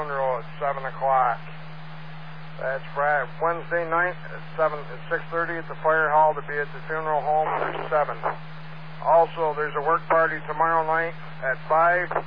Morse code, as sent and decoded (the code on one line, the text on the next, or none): none